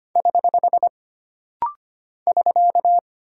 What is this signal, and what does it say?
Morse code, heard as a keyed tone, about 25 wpm, 700 Hz. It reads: <HH> <SK>
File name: Single Punctuation-Prosign - Rapid-Fire - 25wpm